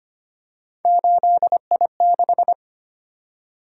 Morse code, heard as a keyed tone, about 25 wpm, 700 Hz. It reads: 8I6
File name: Three Letter-Number Combinations - 25wpm